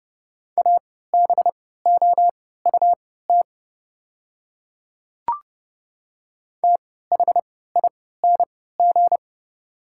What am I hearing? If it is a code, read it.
Morse code, heard as a keyed tone, about 30 wpm, spaced as for 17 wpm, 700 Hz. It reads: ABOUT THING
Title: 3x Character Spacing - Top 100 Words - 30wpm